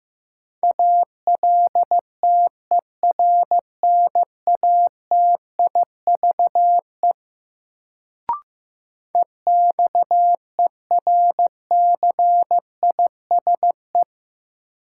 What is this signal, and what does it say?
Morse code, heard as a keyed tone, about 15 wpm, 700 Hz. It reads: ALTERNATIVE EXERCISE
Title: Sentences from Top 1000 Words - Review Part 4 - New Word Drill - 15wpm